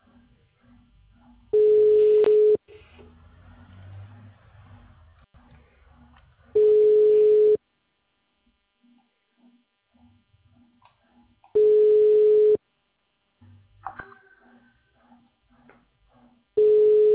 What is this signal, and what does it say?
Ringback tone (Europe, 425 Hz).